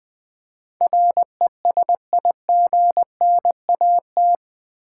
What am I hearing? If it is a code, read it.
Morse code, heard as a keyed tone, about 20 wpm, 700 Hz. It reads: RESIGNAT
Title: Build A Word - 4 Letters At A Time - 11-Letter Words - 20wpm